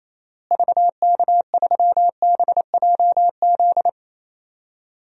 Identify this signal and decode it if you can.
Morse code, heard as a keyed tone, about 28 wpm, 700 Hz. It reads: VK3BJZ
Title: DXCC-100 Call Signs - 6 Characters - Repeated 3x - 28wpm